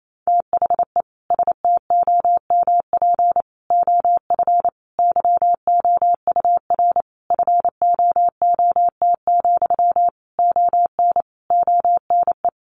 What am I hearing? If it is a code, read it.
Morse code, heard as a keyed tone, about 28 wpm, 700 Hz. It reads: THE STOMP OF YOUR FOOT, ON ONE
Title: A Sound Of Thunder 28wpm